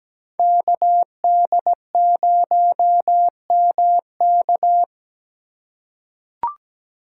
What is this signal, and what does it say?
Morse code, heard as a keyed tone, about 17 wpm, 700 Hz. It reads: KD0MK
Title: USA Call Signs - 5 Characters - 3x Character Spacing - 17wpm